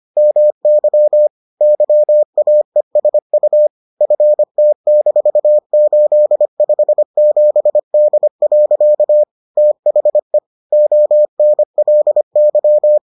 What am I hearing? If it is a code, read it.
Morse code, heard as a keyed tone, about 25 wpm, 600 Hz. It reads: MY YAESU FT-857D. THE ONLY